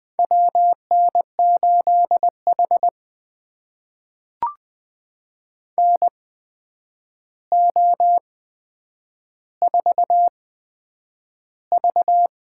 Morse code, heard as a keyed tone, about 20 wpm, spaced as for 9 wpm, 700 Hz. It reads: WN8H NO4V
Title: USA Call Signs - 4 Characters - 8x Character Spacing - 20wpm